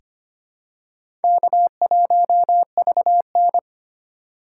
Morse code, heard as a keyed tone, about 25 wpm, 700 Hz. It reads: K1VN
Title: USA Call Signs - 4 Characters - 8x Character Spacing - 25wpm